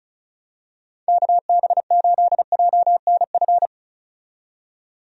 Morse code, heard as a keyed tone, about 34 wpm, 700 Hz. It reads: KB8JNF
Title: USA Call Signs - 6 Characters - 35wpm